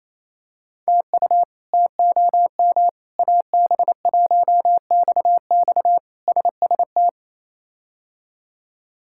Morse code, heard as a keyed tone, about 28 wpm, 700 Hz. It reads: TU TOM AB1XX SST
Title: K1USN SST QSO Practice - 28wpm